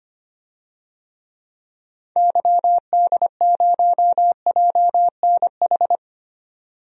Morse code, heard as a keyed tone, about 25 wpm, 700 Hz. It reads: YD0JNH
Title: DXCC-100 Call Signs - 6 Characters - 25wpm